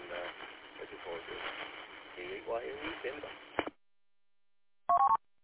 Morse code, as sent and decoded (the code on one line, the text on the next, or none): none